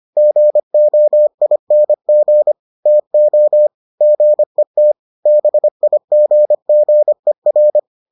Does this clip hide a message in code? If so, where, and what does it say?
Yes: Morse code from 0.2 to 7.8 s, heard as a keyed tone, about 25 wpm, 600 Hz. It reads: GOING TO GET BIGGER